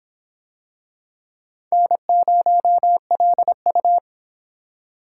Morse code, heard as a keyed tone, about 26 wpm, 700 Hz. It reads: N0LU